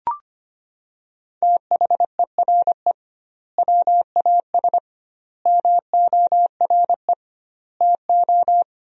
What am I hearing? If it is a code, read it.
Morse code, heard as a keyed tone, about 25 wpm, 700 Hz. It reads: THERE WAS MORE TO
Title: 2x Word-Spacing - Sentences from Top 500 Words - 25wpm